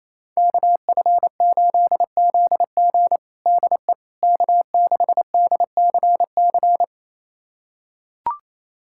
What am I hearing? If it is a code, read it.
Morse code, heard as a keyed tone, about 28 wpm, 700 Hz. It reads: KF8ZG DE K6DCC
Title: USA Callsign DE Callsign - 5 Characters - 28wpm